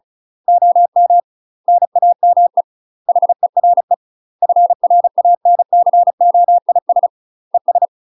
Morse code, heard as a keyed tone, about 34 wpm, 700 Hz. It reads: OM NAME HERE FRANCOIS ES
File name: UK MASTER QSOs TRAINING/35WPM